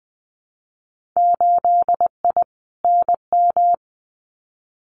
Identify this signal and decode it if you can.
Morse code, heard as a keyed tone, about 19 wpm, 700 Hz. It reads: 8I NM
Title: Winter Field Day Exchanges Encore 20wpm